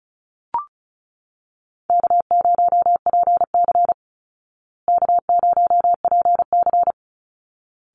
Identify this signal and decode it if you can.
Morse code, heard as a keyed tone, about 34 wpm, 700 Hz. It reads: K0PC K0PC
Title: Easy Call Signs Repeated 2x - 35wpm